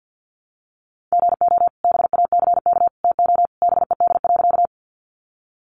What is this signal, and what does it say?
Morse code, heard as a keyed tone, about 50 wpm, 700 Hz. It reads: GO BACK TO BED.
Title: Sentences from Top 500 Words 50wpm